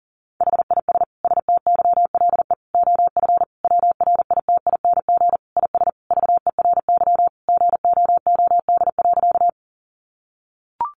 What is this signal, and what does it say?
Morse code, heard as a keyed tone, about 37 wpm, 700 Hz. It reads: HIS STYLE OF WRITING IS VERY GOOD.
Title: Sentences from Top 800 Words 40wpm